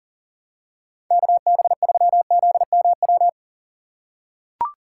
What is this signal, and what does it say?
Morse code, heard as a keyed tone, about 40 wpm, 700 Hz. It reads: KB3ZMW
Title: USA Call Signs - 2x Character Spacing - 40wpm